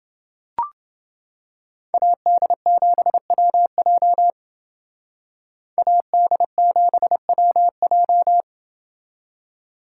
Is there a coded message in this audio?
Yes: Morse code from 1.9 to 8.4 s, heard as a keyed tone, about 29 wpm, 700 Hz. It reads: AD7WJ AD7WJ